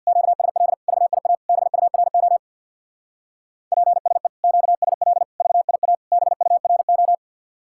Morse code, heard as a keyed tone, about 64 wpm, 700 Hz. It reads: QSL VIA BURO PSE QSL VIA BURO